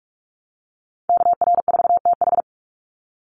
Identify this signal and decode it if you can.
Morse code, heard as a keyed tone, about 44 wpm, 700 Hz. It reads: KR4TH
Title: Call Signs Repeated 3x - 45wpm